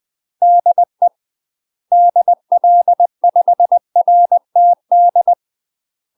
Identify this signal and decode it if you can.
Morse code, heard as a keyed tone, about 20 wpm, 700 Hz. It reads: DE DL5RTD
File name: UK MASTER QSOs TRAINING / 20WPM / UK MASTER QSOs 0058